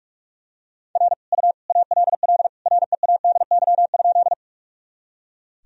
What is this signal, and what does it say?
Morse code, heard as a keyed tone, about 46 wpm, 700 Hz. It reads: R U ALL READY?
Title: Sentences from Top 400 Words and Abbreviations 45wpm